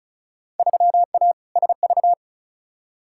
Morse code, heard as a keyed tone, about 36 wpm, 700 Hz. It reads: 3A SV
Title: Field Day Exchanges - 35wpm